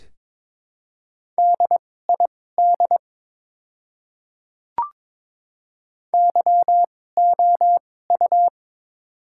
Morse code, heard as a keyed tone, about 22 wpm, spaced as for 16 wpm, 700 Hz. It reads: DID YOU
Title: 2x Character Spacing - Top 100 Words - 22wpm